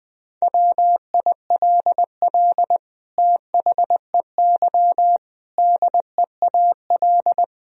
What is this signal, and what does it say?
Morse code, heard as a keyed tone, about 20 wpm, 700 Hz. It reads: WILL THEY DEAL
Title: Sentences from Top 1000 Words - Review Part 4 - 20wpm